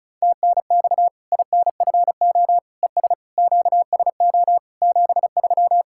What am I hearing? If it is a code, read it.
Morse code, heard as a keyed tone, about 36 wpm, 700 Hz. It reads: TNX INFO ES QSO 73
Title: Practice QSO - Exchange 4 - Ending - 35wpm